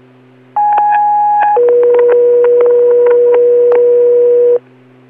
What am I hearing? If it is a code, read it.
Two-tone page, heard as a one-second tone, then a three-second tone, about 830 then 480 Hz. It is heard from 0.6 to 4.6 s.